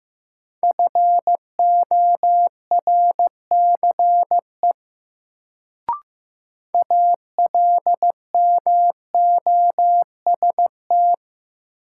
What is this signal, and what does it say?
Morse code, heard as a keyed tone, about 15 wpm, 700 Hz. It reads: FORCE ALMOST
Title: Sentences from Top 700 Words - New Word Drill - 15wpm